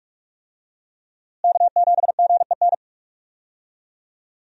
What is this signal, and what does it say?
Morse code, heard as a keyed tone, about 42 wpm, 700 Hz. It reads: K7GEN
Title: USA Call Signs - 5 Characters - Repeated 3x - 45wpm